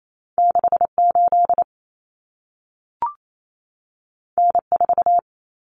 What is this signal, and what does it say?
Morse code, heard as a keyed tone, about 28 wpm, 700 Hz. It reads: 68 N4
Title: Common Cut Numbers - 2-Digit 28wpm